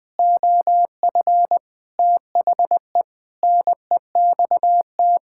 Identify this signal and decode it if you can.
Morse code, heard as a keyed tone, about 20 wpm, 700 Hz. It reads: OF THE NEXT
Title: Sentences with 6-Word Phrases - Spoken Beginning - 20wpm